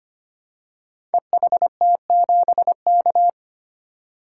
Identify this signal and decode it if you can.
Morse code, heard as a keyed tone, about 25 wpm, 700 Hz. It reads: EHT7K
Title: Code Groups - 5 Letter:Numbers - 25wpm